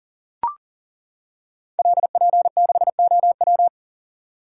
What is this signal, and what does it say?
Morse code, heard as a keyed tone, about 38 wpm, 700 Hz. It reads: LP6OW
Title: Code Groups - 5 Letter:Numbers - 40wpm